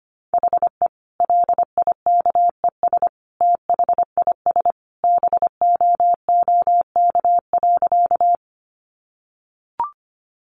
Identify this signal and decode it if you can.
Morse code, heard as a keyed tone, about 25 wpm, 700 Hz. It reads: HE LIKES THIS BOOK.
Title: Additional Easy Phrases 25wpm